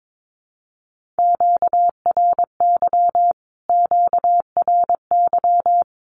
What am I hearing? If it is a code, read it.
Morse code, heard as a keyed tone, about 22 wpm, 700 Hz. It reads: QRY QRY
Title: Q-Codes Repeated 3x 22wpm